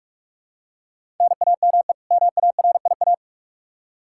Morse code, heard as a keyed tone, about 46 wpm, 700 Hz. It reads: NAME MARIA
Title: Practice QSO - Exchange 1 - Beginning - 45wpm